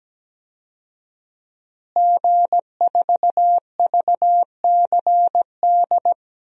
Morse code, heard as a keyed tone, about 17 wpm, 700 Hz. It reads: G4VCD
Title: DXCC-100 Call Signs - 5 Characters - Repeated 2x - 17wpm